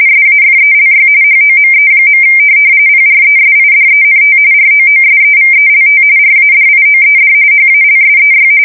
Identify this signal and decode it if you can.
RTTY, about 45.45 baud. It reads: SAFELY HOME. PRESIDENT ROOSEVELT RADIOED HALSEY
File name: fsk170 / USS-New-Jersey